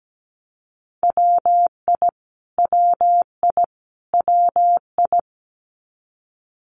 Morse code, heard as a keyed tone, about 17 wpm, 700 Hz. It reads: WI WI WI
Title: US States Repeated 3x 17wpm